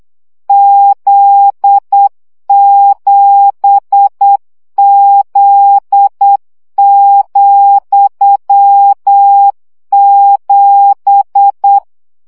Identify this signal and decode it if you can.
Morse code, heard as a keyed tone, about 8 wpm, 800 Hz. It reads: Z7Z,7
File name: CW gyakorlas z 7 vesszo 7 baud